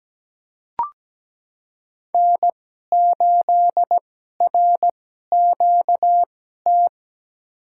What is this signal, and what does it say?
Morse code, heard as a keyed tone, about 17 wpm, spaced as for 12 wpm, 700 Hz. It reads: N8RQT
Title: USA Call Signs - 2x Character Spacing - 17wpm